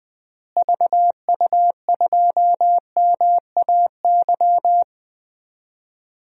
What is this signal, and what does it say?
Morse code, heard as a keyed tone, about 21 wpm, 700 Hz. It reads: VU2MAY
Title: DXCC-100 Call Signs - 6 Characters - Repeated 2x - 20wpm